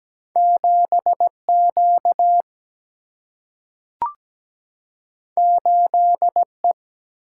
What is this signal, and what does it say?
Morse code, heard as a keyed tone, about 17 wpm, 700 Hz. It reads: 7Q 8E
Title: Single-Letter Single-Number 17wpm